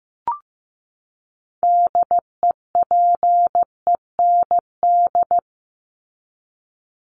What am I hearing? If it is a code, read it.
Morse code, heard as a keyed tone, about 15 wpm, 700 Hz. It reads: DEPEND
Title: Top 1001-1250 Words 15wpm